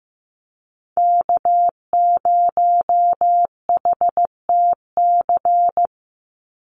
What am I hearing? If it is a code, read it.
Morse code, heard as a keyed tone, about 15 wpm, 700 Hz. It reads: K0HTC
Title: USA Call Signs 15wpm